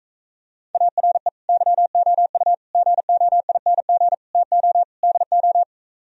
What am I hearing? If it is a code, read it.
Morse code, heard as a keyed tone, about 36 wpm, 700 Hz. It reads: ARE YOU GOING TO DO